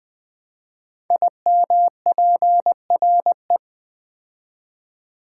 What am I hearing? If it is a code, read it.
Morse code, heard as a keyed tone, about 20 wpm, 700 Hz. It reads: IMPRE
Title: Build A Word - 5 Letters At A Time - 10-Letter Words - 20wpm